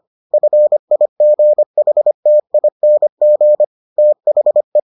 Morse code, heard as a keyed tone, about 25 wpm, 600 Hz. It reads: FIGHTING THE